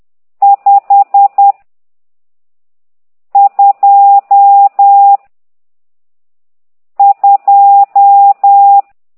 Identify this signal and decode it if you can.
Morse code, heard as a keyed tone, about 10 wpm, 800 Hz. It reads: 5 2 2
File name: ct2idw / exerc12